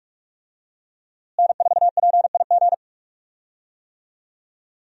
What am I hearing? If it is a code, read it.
Morse code, heard as a keyed tone, about 42 wpm, 700 Hz. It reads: N4PIG
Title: USA Call Signs - 5 Characters - Repeated 2x - 45wpm